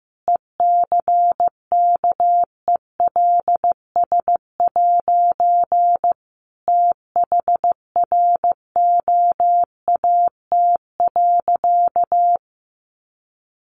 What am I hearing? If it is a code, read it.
Morse code, heard as a keyed tone, about 15 wpm, 700 Hz. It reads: ECKELS' THROAT.
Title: A Sound Of Thunder 15wpm